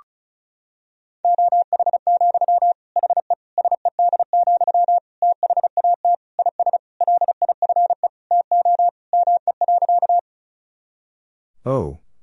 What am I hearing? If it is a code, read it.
Morse code, heard as a keyed tone, about 36 wpm, 700 Hz. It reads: OH, HE SED, THAT IS LIFE TO ME.
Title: Sentences from Top 500 Words and Abbreviations 35wpm